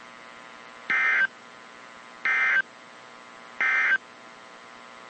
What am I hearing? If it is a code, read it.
Emergency Alert System (SAME) tones, heard as harsh buzzing bursts. It is heard from 0.8 to 4.1 s.
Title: WZ2523 - NOAA Weather Radio